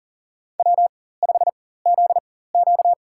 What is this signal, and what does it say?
Morse code, heard as a keyed tone, about 40 wpm, spaced as for 19 wpm, 700 Hz. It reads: W5ZQ
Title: USA Call Signs - 4 Characters - 4x Character Spacing - 40wpm